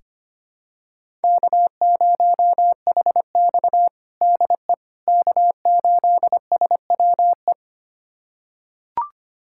Morse code, heard as a keyed tone, about 25 wpm, 700 Hz. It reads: K0HX DE K8SWE
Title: USA Callsign DE Callsign 25wpm